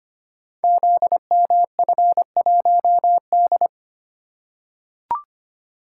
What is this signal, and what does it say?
Morse code, heard as a keyed tone, about 25 wpm, 700 Hz. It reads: ZMF1D
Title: Code Groups - 5 Letter:Numbers - 25wpm